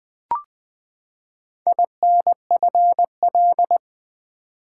Morse code, heard as a keyed tone, about 20 wpm, 700 Hz. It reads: INFL
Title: Build A Word - 4 Letters At A Time - 11-Letter Words - 20wpm